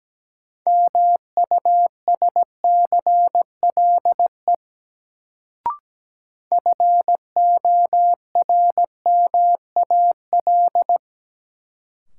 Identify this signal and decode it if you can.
Morse code, heard as a keyed tone, about 17 wpm, 700 Hz. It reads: MUSCLE FORMAL
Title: Sentences from Top 1300 Words - New Word Drill - 17wpm